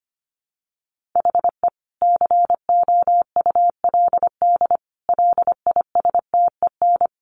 Morse code, heard as a keyed tone, about 25 wpm, 700 Hz. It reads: HE COULD LISTEN
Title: Sentences from Top 1500 Words 25wpm